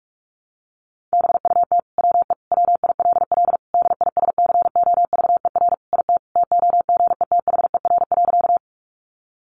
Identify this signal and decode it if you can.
Morse code, heard as a keyed tone, about 44 wpm, 700 Hz. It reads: BUT WE WILL DISCOVER IT TOGETHER.
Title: Sentences from Top 1100 Words 45wpm